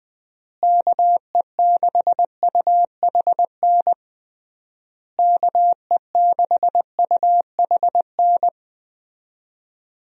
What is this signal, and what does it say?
Morse code, heard as a keyed tone, about 20 wpm, 700 Hz. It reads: KE6UHN KE6UHN
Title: USA Call Signs - 6 Characters - Repeated 3x - 20wpm